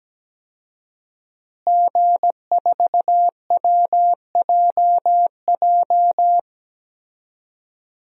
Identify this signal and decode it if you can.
Morse code, heard as a keyed tone, about 17 wpm, 700 Hz. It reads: G4WJJ
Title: DXCC-100 Call Signs - 5 Characters - Repeated 3x - 17wpm